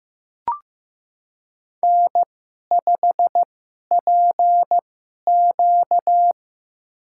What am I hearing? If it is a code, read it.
Morse code, heard as a keyed tone, about 15 wpm, spaced as for 11 wpm, 700 Hz. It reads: N5PQ
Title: USA Call Signs - 2x Character Spacing - 15wpm